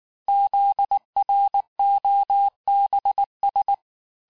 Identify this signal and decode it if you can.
Morse code, heard as a keyed tone, about 19 wpm, 800 Hz. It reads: ZROBS
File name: cours-du-lundi files / cours-1583--texte-chiffre-18-mpm